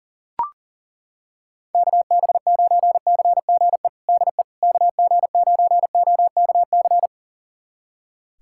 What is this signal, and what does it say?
Morse code, heard as a keyed tone, about 40 wpm, 700 Hz. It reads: KB9CGE DE KG9OKC
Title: USA Callsign DE Callsign - 6 Characters - 40wpm